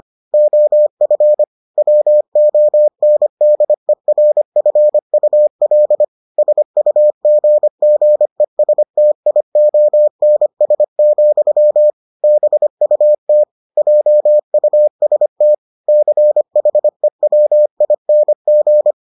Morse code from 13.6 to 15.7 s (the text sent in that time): JUST